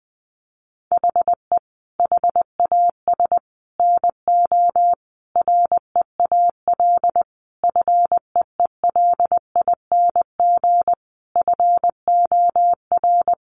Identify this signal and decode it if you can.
Morse code, heard as a keyed tone, about 20 wpm, 700 Hz. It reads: HE HAS NO REAL FEELING FOR
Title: Sentences from Top 1200 Words 20wpm